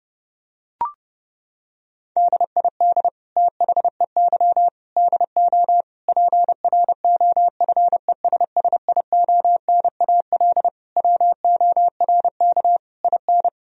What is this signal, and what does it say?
Morse code, heard as a keyed tone, about 30 wpm, 700 Hz. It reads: DID THEY DO PROFESSIONAL WORK IN
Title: Sentences from Top 1000 Words - Review Part 4 - 30wpm